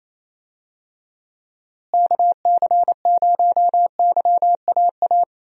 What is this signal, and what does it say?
Morse code, heard as a keyed tone, about 27 wpm, 700 Hz. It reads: KC0YAA
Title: USA Call Signs - 6 Characters - Repeated 3x - 28wpm